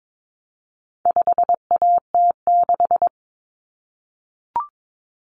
Morse code, heard as a keyed tone, about 22 wpm, 700 Hz. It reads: 5AT6
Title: Common Cut Numbers - 4-Digit 22wpm